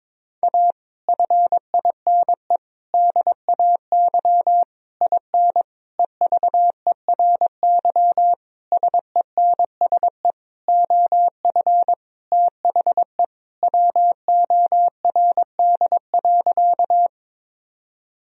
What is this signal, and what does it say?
Morse code, heard as a keyed tone, about 21 wpm, 700 Hz. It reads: A FINE DAY IN EVERY SENSE OF THE WORD.
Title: Sentences from Top 1000 Words - Review Part 4 - 22wpm